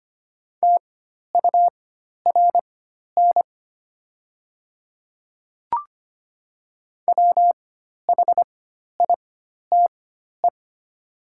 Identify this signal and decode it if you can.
Morse code, heard as a keyed tone, about 25 wpm, spaced as for 12 wpm, 700 Hz. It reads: TURN WHITE
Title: 4x Character Spacing - Top 500 Words - 25wpm